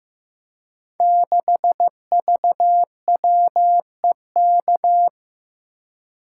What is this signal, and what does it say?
Morse code, heard as a keyed tone, about 15 wpm, 700 Hz. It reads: 6VWEK
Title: Code Groups - 5 Letter:Numbers - 15wpm